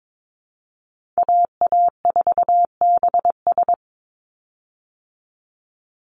Morse code, heard as a keyed tone, about 22 wpm, 700 Hz. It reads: AA4BS
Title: USA Call Signs 22wpm